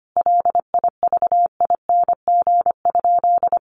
Morse code, heard as a keyed tone, about 25 wpm, 700 Hz. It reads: LIVING?
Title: Sentences from Top 1100 Words 25wpm